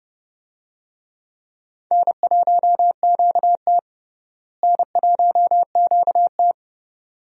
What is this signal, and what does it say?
Morse code, heard as a keyed tone, about 29 wpm, 700 Hz. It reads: N1QT N1QT
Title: USA Call Signs - 4 Characters - Repeated 2x - 30wpm